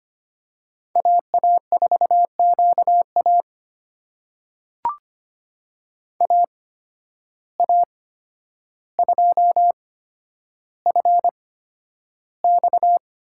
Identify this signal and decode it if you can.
Morse code, heard as a keyed tone, about 25 wpm, spaced as for 11 wpm, 700 Hz. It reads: AA4QA AA2FX
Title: USA Call Signs - 5 Characters - 8x Character Spacing - 25wpm